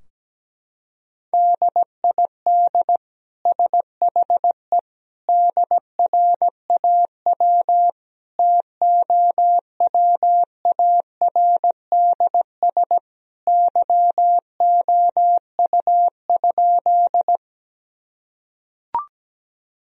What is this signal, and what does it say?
Morse code, heard as a keyed tone, about 17 wpm, 700 Hz. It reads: DID SHE DRAW TOWARDS YOU?
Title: Sentences from Top 1000 Words - Review Part 2 - 17wpm